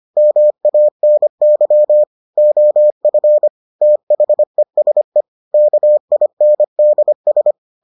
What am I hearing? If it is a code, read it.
Morse code, heard as a keyed tone, about 25 wpm, 600 Hz. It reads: MANY OF THESE KINDS